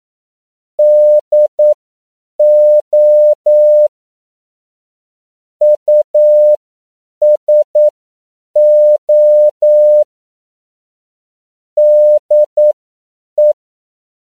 Morse code, heard as a keyed tone, about 9 wpm, spaced as for 7 wpm, 600 Hz. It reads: DO USO DE
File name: CW / 10PPM